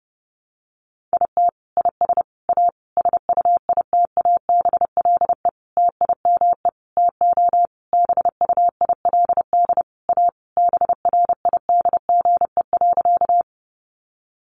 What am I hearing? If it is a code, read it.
Morse code, heard as a keyed tone, about 30 wpm, 700 Hz. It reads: IT IS A SUITABLE TIME TO BUILD A BRIDGE.
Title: Sentences from Top 1400 Words 30wpm